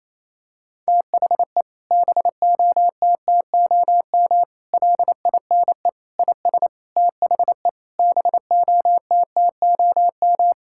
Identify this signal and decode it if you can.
Morse code, heard as a keyed tone, about 28 wpm, 700 Hz. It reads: THE BOTTOM LINE IS THE BOTTOM
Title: Sentences from Top 1000 Words - Review Part 6 - 28wpm